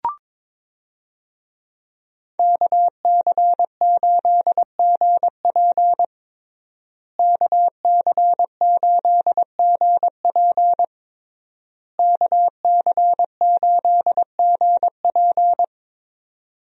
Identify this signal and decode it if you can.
Morse code, heard as a keyed tone, about 22 wpm, 700 Hz. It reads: KC8GP KC8GP KC8GP
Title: USA Call Signs - 5 Characters - Repeated 3x - 22wpm